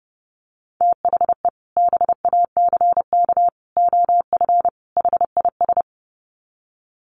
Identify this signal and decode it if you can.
Morse code, heard as a keyed tone, about 30 wpm, 700 Hz. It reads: THE BACK OF HIS
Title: Sets of 4 Words 30wpm